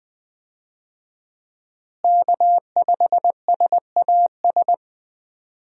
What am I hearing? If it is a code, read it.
Morse code, heard as a keyed tone, about 20 wpm, 700 Hz. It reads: K5SAS
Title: USA Call Signs - 5 Characters - Repeated 3x - 20wpm